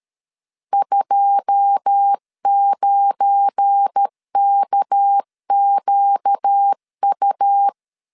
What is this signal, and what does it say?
Morse code, heard as a keyed tone, about 13 wpm, 800 Hz. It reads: 29KQU
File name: CW / DET / 1 9 1500C S A 12wpm 3800